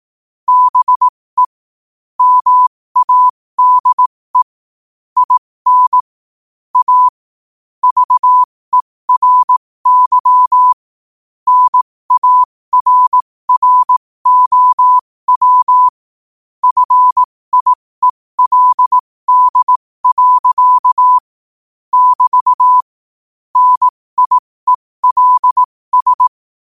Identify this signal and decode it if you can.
Morse code, heard as a keyed tone, about 18 wpm, spaced as for 15 wpm, 1000 Hz. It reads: BE MADE IN A VERY NARROW FIELD. <BT> NIELS
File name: qotdfiles / 15wpm